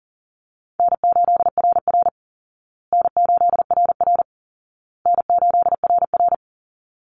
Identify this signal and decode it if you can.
Morse code, heard as a keyed tone, about 40 wpm, 700 Hz. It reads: N8RR N8RR N8RR
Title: Call Signs Repeated 2x - 40wpm